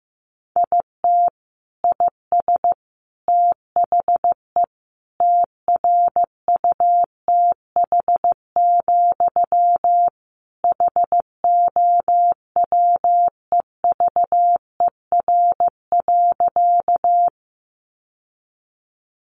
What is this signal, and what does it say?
Morse code, heard as a keyed tone, about 15 wpm, 700 Hz. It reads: IT IS THE TRUTH, HOWEVER.
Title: Sentences from Top 800 Words 15wpm